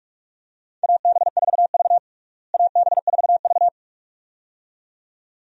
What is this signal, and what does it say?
Morse code, heard as a keyed tone, about 41 wpm, 700 Hz. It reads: AB4V AB4V
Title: USA Call Signs - 4 Characters - Repeated 2x - 45wpm